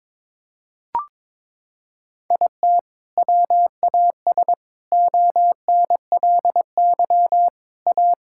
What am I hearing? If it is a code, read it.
Morse code, heard as a keyed tone, about 22 wpm, 700 Hz. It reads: IT WAS ONLY A
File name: practice / 22wpm / Sets of 8 Words - 22wpm